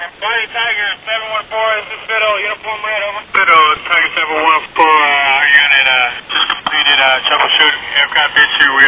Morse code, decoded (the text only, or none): none